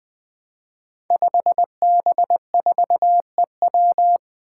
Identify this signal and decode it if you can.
Morse code, heard as a keyed tone, about 20 wpm, 700 Hz. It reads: 5B4EW
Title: Five Challenging Letter-Number Combinations 20wpm